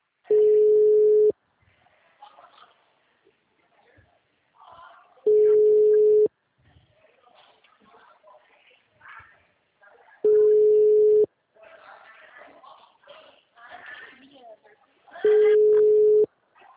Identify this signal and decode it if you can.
Ringback tone (Europe, 425 Hz), with speech.
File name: out- -2018- - - .6